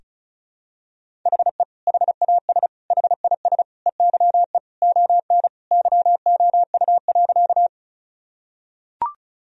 Morse code, heard as a keyed tone, about 34 wpm, 700 Hz. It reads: HE HAS HIS EYE ON YOU.
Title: Sentences from Top 1000 Words - Review Part 1 - 35wpm